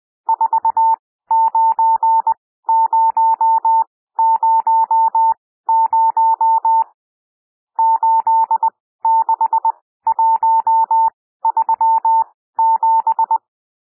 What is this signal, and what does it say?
Morse code, heard as a keyed tone, about 20 wpm, spaced as for 14 wpm, 900 Hz. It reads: 49000 86137
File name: cwmp3 / N1 SOUND NUMBERS 20 wpm